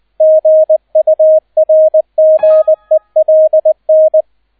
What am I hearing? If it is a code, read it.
Morse code, heard as a keyed tone, about 20 wpm, 630 Hz, with radio noise. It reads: GURGELN
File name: 002 Zweiter Streich 20wpm